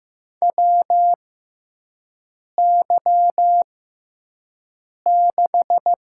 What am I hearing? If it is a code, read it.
Morse code, heard as a keyed tone, about 15 wpm, spaced as for 5 wpm, 700 Hz. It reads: WY6
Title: USA Call Signs - 4 Characters - 6x Character Spacing - 15wpm